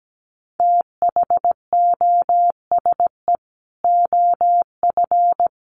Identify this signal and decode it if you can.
Morse code, heard as a keyed tone, about 17 wpm, 700 Hz. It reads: THOSE OF
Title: Sets of 5 Words Encore2 17wpm